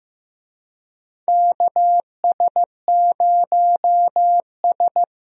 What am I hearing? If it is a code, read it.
Morse code, heard as a keyed tone, about 15 wpm, 700 Hz. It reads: KS0S
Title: USA Call Signs - 4 Characters - Repeated 2x - 15wpm